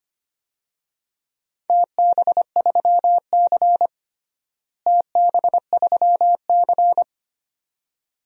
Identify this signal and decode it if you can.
Morse code, heard as a keyed tone, about 25 wpm, 700 Hz. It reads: TB3C TB3C
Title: DXCC-100 Call Signs - 4 Characters - Repeated 2x - 25wpm